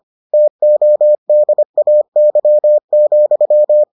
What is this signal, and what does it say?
Morse code, heard as a keyed tone, about 25 wpm, 600 Hz. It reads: TODAY,